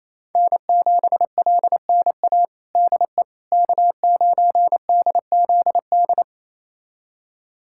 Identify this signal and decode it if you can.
Morse code, heard as a keyed tone, about 28 wpm, 700 Hz. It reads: N7LNA DE K9DZD
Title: USA Callsign DE Callsign - 5 Characters - 28wpm